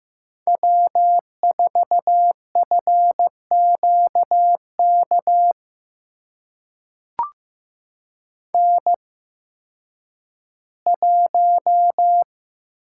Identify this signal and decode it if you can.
Morse code, heard as a keyed tone, about 15 wpm, spaced as for 10 wpm, 700 Hz. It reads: W4FQK N1
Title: USA Call Signs - 5 Characters - 8x Character Spacing - 15wpm